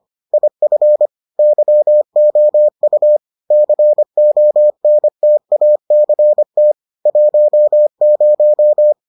Morse code, heard as a keyed tone, about 25 wpm, 600 Hz. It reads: IF YOU CONTACT 10